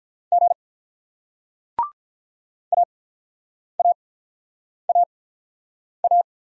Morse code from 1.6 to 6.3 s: AAAA

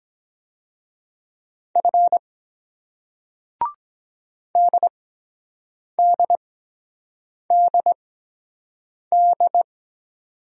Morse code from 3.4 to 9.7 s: DDDD

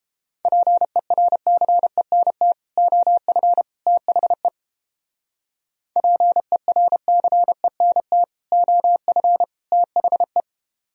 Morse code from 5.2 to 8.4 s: PERCENT

